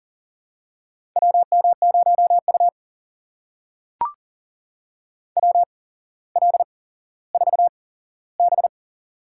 Morse code, heard as a keyed tone, about 38 wpm, spaced as for 17 wpm, 700 Hz. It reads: WM0U WL4B
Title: USA Call Signs - 4 Characters - 8x Character Spacing - 40wpm